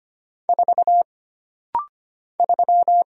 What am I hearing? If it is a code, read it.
Morse code, heard as a keyed tone, about 25 wpm, 700 Hz. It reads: 4 3
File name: Single Character (Letter:Number:Punctuation:Prosign) - Rapid-Fire - 25wpm